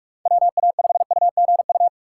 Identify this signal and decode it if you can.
Morse code, heard as a keyed tone, about 59 wpm, spaced as for 52 wpm, 700 Hz. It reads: WA5UGU
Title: Speed Racing - USA 2x3 Call Signs 45wpm